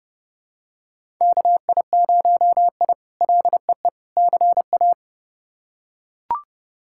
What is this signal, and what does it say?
Morse code, heard as a keyed tone, about 30 wpm, 700 Hz. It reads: KI0I LEE CA
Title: K1USN SST QSO Practice - 30wpm